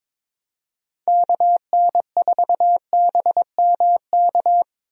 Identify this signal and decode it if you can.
Morse code, heard as a keyed tone, about 22 wpm, 700 Hz. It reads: KN4BMK
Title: USA Call Signs - 6 Characters - Repeated 3x - 22wpm